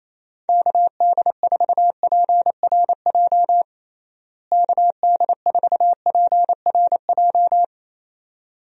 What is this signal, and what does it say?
Morse code, heard as a keyed tone, about 28 wpm, 700 Hz. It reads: KD4PRJ KD4PRJ
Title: USA Call Signs - 6 Characters - Repeated 2x - 28wpm